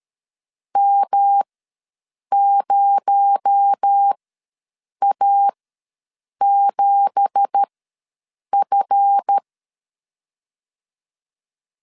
Morse code, heard as a keyed tone, about 13 wpm, spaced as for 7 wpm, 800 Hz. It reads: M0A7F
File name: LEZIONI / 10 SIGNS / 10A SLASH ws 12wpm 3443